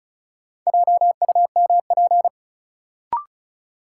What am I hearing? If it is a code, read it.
Morse code, heard as a keyed tone, about 36 wpm, 700 Hz. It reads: JUMP
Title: Sentences from Top 1300 Words - New Word Drill - 35wpm